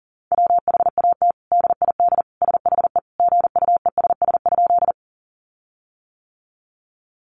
Morse code, heard as a keyed tone, about 40 wpm, 700 Hz. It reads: WHAT DID SHE GUESS?
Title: Sentences from Top 1500 Words 40wpm